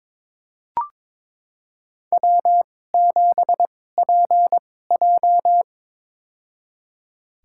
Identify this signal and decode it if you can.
Morse code, heard as a keyed tone, about 22 wpm, spaced as for 16 wpm, 700 Hz. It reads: W7PJ
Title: USA Call Signs - 4 Characters - 2x Character Spacing - 22wpm